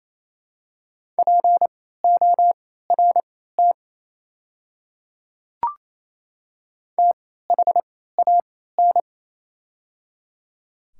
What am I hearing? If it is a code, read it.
Morse code, heard as a keyed tone, about 28 wpm, spaced as for 16 wpm, 700 Hz. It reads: PORT THAN
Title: 3x Character Spacing - Top 300 Words - 28wpm